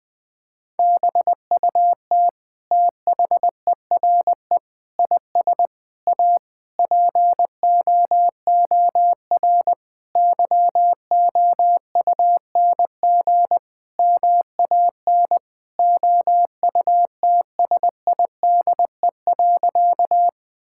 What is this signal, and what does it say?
Morse code, heard as a keyed tone, about 21 wpm, 700 Hz. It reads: BUT THERE IS A POOR YOUNG MAN OUTSIDE.
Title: Sentences from Top 1000 Words - Review Part 6 - 20wpm